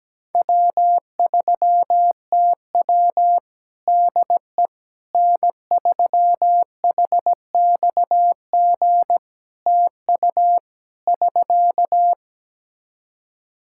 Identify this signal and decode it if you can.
Morse code, heard as a keyed tone, about 17 wpm, 700 Hz. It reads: W3TW DE N3HXG TU <SK>
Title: Practice QSO - Exchange 4 - Ending - 17wpm